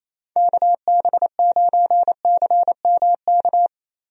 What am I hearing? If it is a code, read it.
Morse code, heard as a keyed tone, about 28 wpm, 700 Hz. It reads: KB9CMK
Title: USA Call Signs - 6 Characters - Repeated 2x - 28wpm